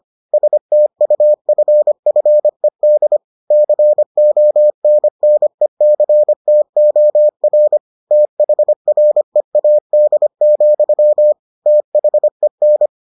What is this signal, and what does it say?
Morse code, heard as a keyed tone, about 25 wpm, 600 Hz. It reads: STUFFED CONNECTOR THREAD, THEN